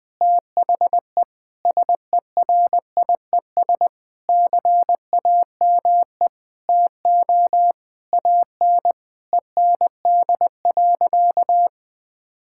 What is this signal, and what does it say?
Morse code, heard as a keyed tone, about 20 wpm, 700 Hz. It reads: THE SERIES CAME TO AN END.
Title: Sentences from Top 1000 Words - Review Part 5 - 20wpm